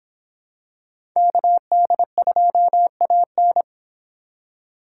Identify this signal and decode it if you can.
Morse code, heard as a keyed tone, about 25 wpm, 700 Hz. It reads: KD2AN